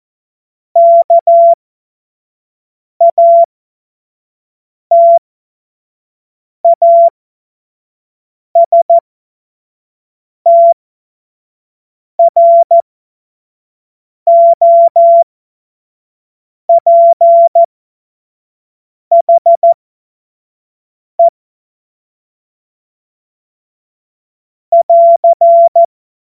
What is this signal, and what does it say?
Morse code, heard as a keyed tone, about 14 wpm, spaced as for 5 wpm, 680 Hz. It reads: KATASTROPHE <AR>
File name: Morsekurs / Witze / Witz 15. Juni 2021, 25 F70